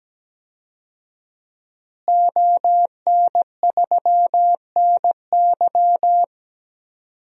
Morse code, heard as a keyed tone, about 17 wpm, 700 Hz. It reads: ON3NY
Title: DXCC-100 Call Signs - 5 Characters - Repeated 3x - 17wpm